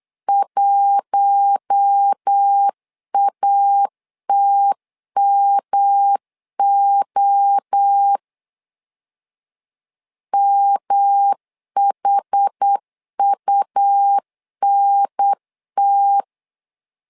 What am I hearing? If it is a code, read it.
Morse code, heard as a keyed tone, about 8 wpm, 800 Hz. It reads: 1ATMO MHUNT
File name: LEZIONI / 4 Q1 8wpm 3035